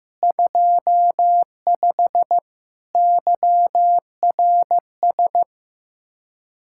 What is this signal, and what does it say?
Morse code, heard as a keyed tone, about 15 wpm, 700 Hz. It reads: 25 YRS
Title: Speed-Racing Common QSO Phrases Encore - 15wpm